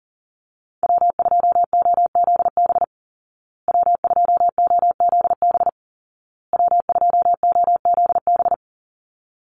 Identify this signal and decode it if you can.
Morse code, heard as a keyed tone, about 40 wpm, 700 Hz. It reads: W2OZB W2OZB W2OZB
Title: Call Signs Repeated 2x - 40wpm